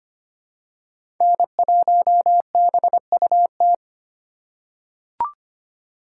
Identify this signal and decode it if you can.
Morse code, heard as a keyed tone, about 25 wpm, 700 Hz. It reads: N1BUT
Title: USA Call Signs - 4x Character Spacing - 25wpm